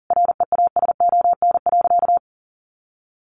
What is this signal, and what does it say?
Morse code, heard as a keyed tone, about 37 wpm, 700 Hz. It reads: REASON.
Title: Sentences from Top 1300 Words 40wpm